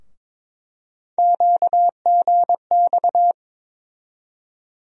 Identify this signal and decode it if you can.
Morse code, heard as a keyed tone, about 22 wpm, 700 Hz. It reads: QGX
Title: USA Call Sign Postfixes - 3 Letters 22wpm